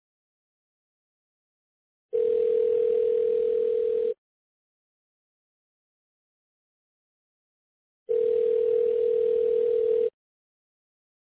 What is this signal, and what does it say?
Ringback tone (North America), 440 + 480 Hz.